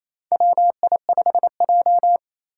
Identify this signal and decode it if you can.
Morse code, heard as a keyed tone, about 27 wpm, 700 Hz. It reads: WI5J
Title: Four Challenging Letter-Number Combinations 28wpm